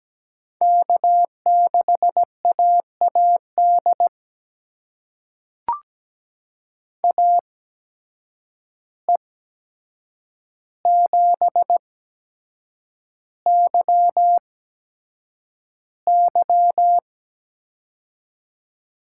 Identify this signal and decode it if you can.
Morse code, heard as a keyed tone, about 17 wpm, spaced as for 7 wpm, 700 Hz. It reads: K6AAD AE7YY
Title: USA Call Signs - 5 Characters - 8x Character Spacing - 17wpm